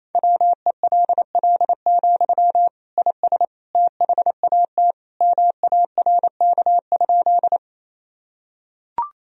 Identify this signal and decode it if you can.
Morse code, heard as a keyed tone, about 28 wpm, 700 Hz. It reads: WELL, IS THAT MARK?
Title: Sentences from Top 1000 Words - Review Part 1 - 28wpm